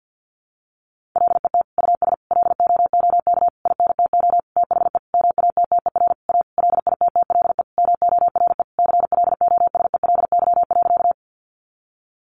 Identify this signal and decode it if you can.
Morse code, heard as a keyed tone, about 50 wpm, 700 Hz. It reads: LET US LOOK INTO THE MATTER A LITTLE MORE CLOSELY.
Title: Sentences from Top 1400 Words 50wpm